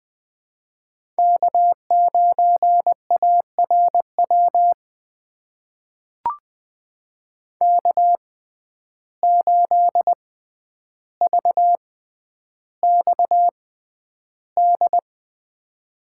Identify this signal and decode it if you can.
Morse code, heard as a keyed tone, about 20 wpm, spaced as for 10 wpm, 700 Hz. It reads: K9ARW K8VXD
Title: USA Call Signs - 5 Characters - 6x Character Spacing - 20wpm